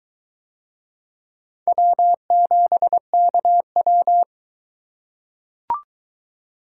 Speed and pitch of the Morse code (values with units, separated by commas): 21 wpm, 700 Hz